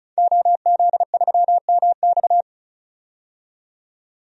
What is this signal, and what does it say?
Morse code, heard as a keyed tone, about 36 wpm, 700 Hz. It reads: OZ3MX
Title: DXCC-100 Call Signs - 5 Characters - 35wpm